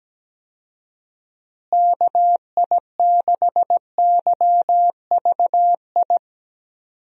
Morse code, heard as a keyed tone, about 17 wpm, 700 Hz. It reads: KI6YVI
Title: USA Call Signs - 6 Characters - Repeated 2x - 17wpm